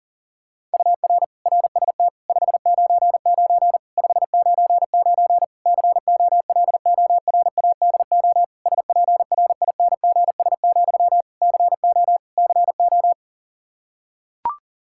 Morse code, heard as a keyed tone, about 40 wpm, 700 Hz. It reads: UR RST 599 599 COLORADO SPRINGS, CO CO
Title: K1USN SST QSO Practice 40wpm